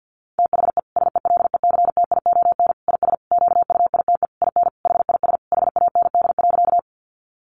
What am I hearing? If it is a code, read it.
Morse code, heard as a keyed tone, about 50 wpm, 700 Hz. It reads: THE SELECTION IS QUITE IN HIS HAND.
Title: Sentences from Top 1400 Words 50wpm